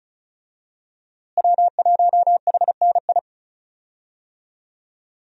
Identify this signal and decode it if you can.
Morse code, heard as a keyed tone, about 36 wpm, 700 Hz. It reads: W1HNI
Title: USA Call Signs - 5 Characters - Repeated 2x - 35wpm